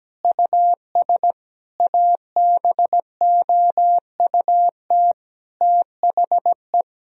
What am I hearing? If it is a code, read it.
Morse code, heard as a keyed tone, about 17 wpm, 700 Hz. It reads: US ABOUT THE
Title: Sentences from Top 1000 Words - Review Part 3 - 17wpm